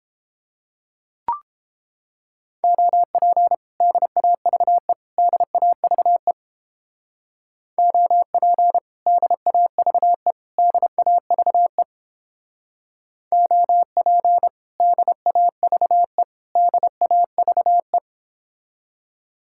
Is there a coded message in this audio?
Yes: Morse code from 2.6 to 18.0 s, heard as a keyed tone, about 29 wpm, 700 Hz. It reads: OP DAVE DAVE OP DAVE DAVE OP DAVE DAVE